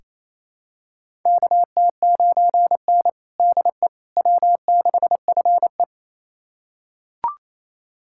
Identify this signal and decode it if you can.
Morse code, heard as a keyed tone, about 28 wpm, 700 Hz. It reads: KT9N DE W6FE
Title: USA Callsign DE Callsign - 4 Characters - 28wpm